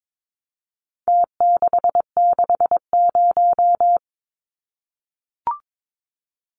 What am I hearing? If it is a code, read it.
Morse code, heard as a keyed tone, about 22 wpm, 700 Hz. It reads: T660
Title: Numbers - Contest Style - 4-Digit 22wpm